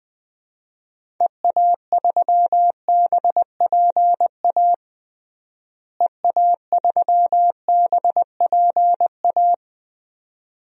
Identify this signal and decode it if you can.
Morse code, heard as a keyed tone, about 20 wpm, 690 Hz. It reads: EA3BPA EA3BPA
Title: DXCC-100 Call Signs - 6 Characters - Repeated 2x - 20wpm